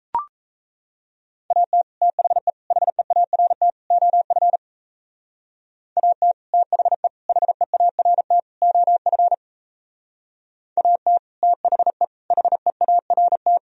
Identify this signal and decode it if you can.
Morse code, heard as a keyed tone, about 35 wpm, 700 Hz. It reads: AT THE HEART OF AT THE HEART OF AT THE HEART
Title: Speed-Racing - Sets of 4 Words 28wpm